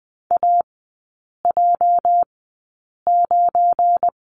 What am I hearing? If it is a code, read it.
Morse code, heard as a keyed tone, about 20 wpm, spaced as for 8 wpm, 690 Hz. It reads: AJ9
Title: 205 Lesson 33e - Letter j - Three Characters 20wpm